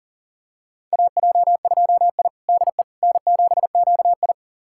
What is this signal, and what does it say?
Morse code, heard as a keyed tone, about 40 wpm, 700 Hz. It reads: AJ2I DE N7QI
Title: USA Callsign DE Callsign - 4 Characters - 40wpm